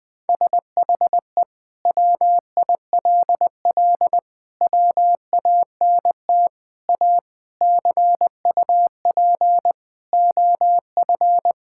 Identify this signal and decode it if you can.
Morse code, heard as a keyed tone, about 20 wpm, 700 Hz. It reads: SHE WILL WANT A CUP OF
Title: Sentences from Top 1600 Words 20wpm